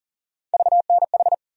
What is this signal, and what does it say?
Morse code, heard as a keyed tone, about 40 wpm, 700 Hz. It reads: VNH
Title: USA Call Sign Postfixes - 3 Letters 40wpm